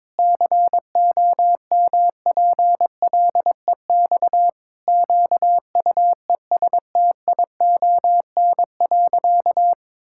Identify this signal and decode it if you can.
Morse code, heard as a keyed tone, about 22 wpm, 700 Hz. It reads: COMPLEX QUESTION.
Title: Sentences from Top 1000 Words - Review Part 6 - 22wpm